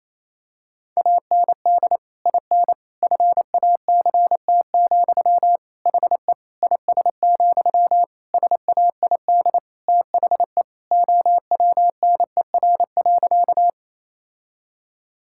Morse code, heard as a keyed tone, about 28 wpm, 700 Hz. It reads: AND IN FACT, HE IS, SAID THE OWNER.
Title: Sentences from Top 1600 Words 28wpm